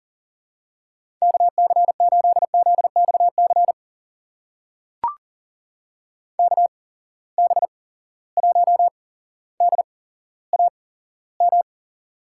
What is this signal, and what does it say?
Morse code, heard as a keyed tone, about 40 wpm, spaced as for 17 wpm, 700 Hz. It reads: KC8ZXC KB1DAM
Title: USA Call Signs - 6 Characters - 8x Character Spacing - 40wpm